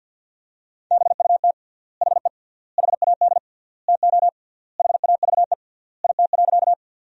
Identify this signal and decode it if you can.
Morse code, heard as a keyed tone, about 50 wpm, 700 Hz. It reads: BUT HE HAD TO HAVE IT.
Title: 3x Word-Spacing - Sentences from Top 100 Words - 50wpm